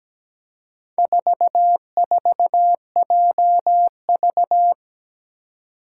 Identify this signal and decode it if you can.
Morse code, heard as a keyed tone, about 17 wpm, 700 Hz. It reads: 44JV
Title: Four Challenging Letter-Number Combinations 17wpm